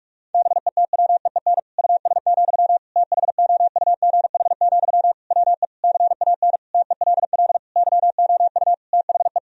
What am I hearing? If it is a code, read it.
Morse code, heard as a keyed tone, about 46 wpm, 700 Hz. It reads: BETWEEN US, THOUGH, WE CAN TELL YOU THE
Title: Sentences from Top 1000 Words - Review Part 2 - 45wpm